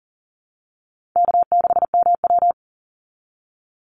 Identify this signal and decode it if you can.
Morse code, heard as a keyed tone, about 39 wpm, 700 Hz. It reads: K6MW
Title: NA SOTA Chaser Callsigns 40wpm